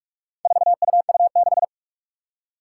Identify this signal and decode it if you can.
Morse code, heard as a keyed tone, about 44 wpm, 700 Hz. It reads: 4UU6
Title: Four Challenging Letter-Number Combinations 45wpm